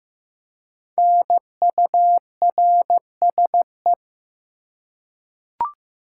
Morse code, heard as a keyed tone, about 15 wpm, 700 Hz. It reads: NURSE
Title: Five Letter Words - 15wpm